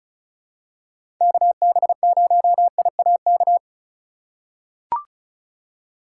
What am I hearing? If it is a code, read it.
Morse code, heard as a keyed tone, about 36 wpm, 700 Hz. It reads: KB0IAK
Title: USA Call Signs - 6 Characters - 3x Character Spacing - 35wpm